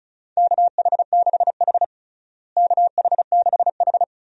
Morse code, heard as a keyed tone, about 36 wpm, 700 Hz. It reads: KH6H KH6H
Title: DXCC-100 Call Signs - 4 Characters - Repeated 2x - 35wpm